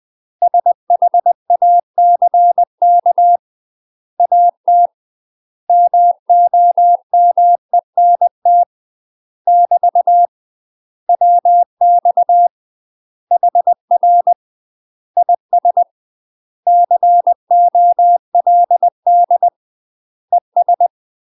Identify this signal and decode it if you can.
Morse code, heard as a keyed tone, about 20 wpm, 700 Hz. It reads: SHACK AT MOMENT <BT> WX HR IS COLD ES
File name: UK MASTER QSOs TRAINING / 20WPM / UK MASTER QSOs 0011